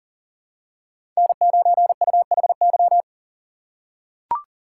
Morse code, heard as a keyed tone, about 40 wpm, 700 Hz. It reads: N9UHY
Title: USA Call Signs - 5 Characters - 3x Character Spacing - 40wpm